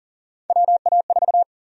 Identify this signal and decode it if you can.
Morse code, heard as a keyed tone, about 40 wpm, 700 Hz. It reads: WA4
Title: USA Call Sign Prefixes - Repeated 2x - 40wpm